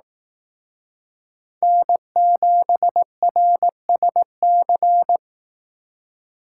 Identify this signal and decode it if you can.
Morse code, heard as a keyed tone, about 18 wpm, 700 Hz. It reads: N7RSC
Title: Speed Racing - USA Call Signs 15wpm